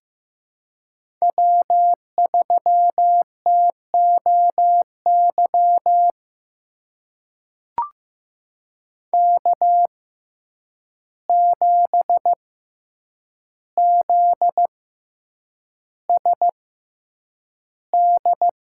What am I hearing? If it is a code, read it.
Morse code, heard as a keyed tone, about 15 wpm, spaced as for 8 wpm, 700 Hz. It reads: W3TOY K7ZSD
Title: USA Call Signs - 5 Characters - 6x Character Spacing - 15wpm